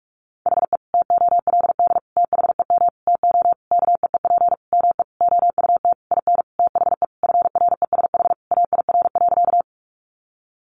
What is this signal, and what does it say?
Morse code, heard as a keyed tone, about 44 wpm, 700 Hz. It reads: HE TOLD THEM TO KEEP ME OUT IN THE FRESH AIR.
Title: Sentences from Top 1200 Words 45wpm